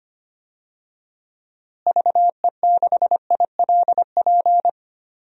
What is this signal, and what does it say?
Morse code, heard as a keyed tone, about 25 wpm, 700 Hz. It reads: VE6ILP
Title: DXCC-100 Call Signs - 6 Characters - Repeated 2x - 25wpm